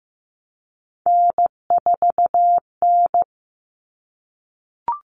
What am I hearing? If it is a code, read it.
Morse code, heard as a keyed tone, about 15 wpm, 700 Hz. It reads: N4N
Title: All Cut Numbers - 3-Digit 15wpm